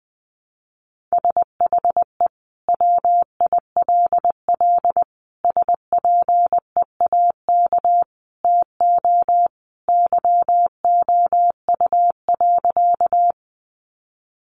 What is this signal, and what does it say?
Morse code, heard as a keyed tone, about 20 wpm, 700 Hz. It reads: SHE WILL SPEAK TO YOU.
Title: Sentences from Top 1100 Words 20wpm